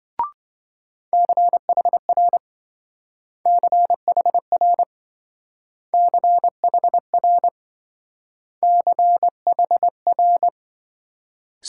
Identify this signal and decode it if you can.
Morse code, heard as a keyed tone, about 25 wpm, 700 Hz. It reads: CHR CHR CHR CHR